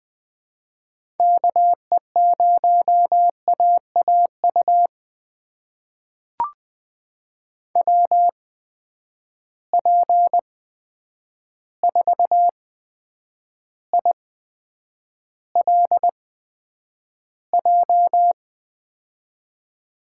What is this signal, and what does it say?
Morse code, heard as a keyed tone, about 20 wpm, spaced as for 9 wpm, 700 Hz. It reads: KE0AAU WP4ILJ
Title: USA Call Signs - 6 Characters - 8x Character Spacing - 20wpm